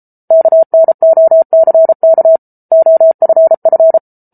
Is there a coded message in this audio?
Yes: Morse code from 0.3 to 4.0 s, heard as a keyed tone, about 34 wpm, 650 Hz. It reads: KNOCK OFF